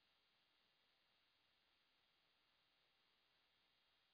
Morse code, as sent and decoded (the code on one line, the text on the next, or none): none